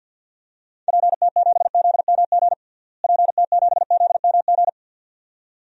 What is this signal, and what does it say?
Morse code, heard as a keyed tone, about 49 wpm, 700 Hz. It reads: PT7ZMG PT7ZMG
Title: DXCC-100 Call Signs - 6 Characters - Repeated 2x - 50wpm